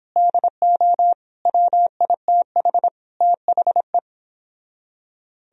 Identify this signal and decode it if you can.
Morse code, heard as a keyed tone, about 29 wpm, 700 Hz. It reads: DO WITH THE